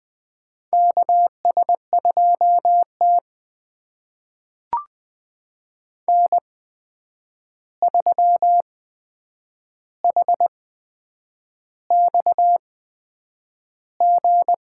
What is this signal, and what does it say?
Morse code, heard as a keyed tone, about 20 wpm, spaced as for 8 wpm, 700 Hz. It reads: KS2T N3HXG
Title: USA Call Signs - 8x Character Spacing - 20wpm